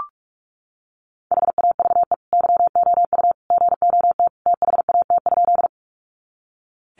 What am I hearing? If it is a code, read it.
Morse code, heard as a keyed tone, about 44 wpm, 700 Hz. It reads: HAVE YOU GOT THAT?
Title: Sentences from Top 300 Words 45wpm